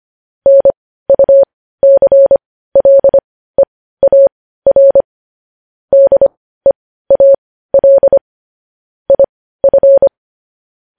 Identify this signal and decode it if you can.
Morse code, heard as a keyed tone, about 25 wpm, spaced as for 15 wpm, 550 Hz. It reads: NUCLEAR DEAL IF
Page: news-headlines-in-morse-code-at-15-wpm